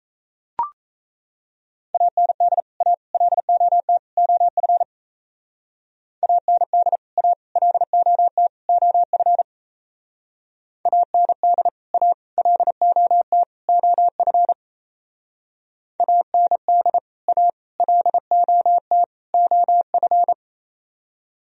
Morse code, heard as a keyed tone, about 35 wpm, 700 Hz. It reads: AND A LOT OF AND A LOT OF AND A LOT OF AND A LOT OF
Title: Speed-Racing - Sets of 4 Words 28wpm